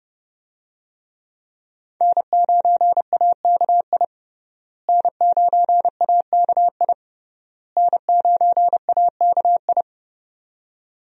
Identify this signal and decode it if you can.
Morse code, heard as a keyed tone, about 30 wpm, 700 Hz. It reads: N9AKI N9AKI N9AKI
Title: USA Call Signs - 5 Characters - Repeated 3x - 30wpm